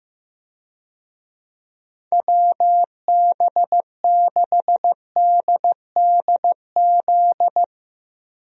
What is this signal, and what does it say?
Morse code, heard as a keyed tone, about 16 wpm, 700 Hz. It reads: WB6DDZ
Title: USA Call Signs - 6 Characters - Repeated 3x - 15wpm